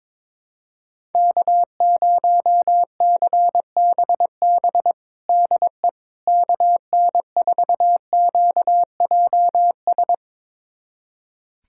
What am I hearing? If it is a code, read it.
Morse code, heard as a keyed tone, about 22 wpm, 700 Hz. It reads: K0CBB DE KN4QJS
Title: USA Callsign DE Callsign 22wpm